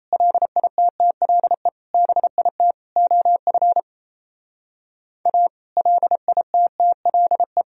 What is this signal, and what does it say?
Morse code, heard as a keyed tone, about 34 wpm, 700 Hz. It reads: LITTLE BIT OF A LITTLE
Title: Speed-Racing - Sets of 4 Words 28wpm